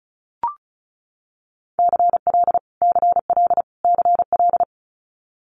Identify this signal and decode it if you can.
Morse code, heard as a keyed tone, about 34 wpm, 700 Hz. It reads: CL CL CL
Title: Multi-letter Phonemes Repeated 3x 35wpm